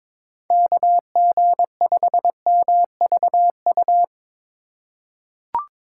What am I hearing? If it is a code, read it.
Morse code, heard as a keyed tone, about 21 wpm, 700 Hz. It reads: KG5MVU
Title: USA Call Signs - 6 Characters - 2x Character Spacing - 22wpm